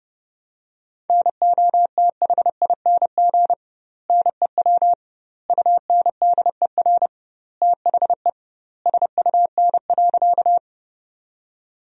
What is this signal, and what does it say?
Morse code, heard as a keyed tone, about 30 wpm, 700 Hz. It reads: NOTHING NEW UNDER THE SUN.
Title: 2x Word-Spacing - Sentences from Top 500 Words - 30wpm